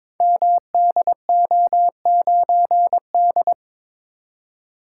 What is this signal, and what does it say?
Morse code, heard as a keyed tone, about 22 wpm, 700 Hz. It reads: MDO9D
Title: Code Groups - 5 Letter:Numbers - 22wpm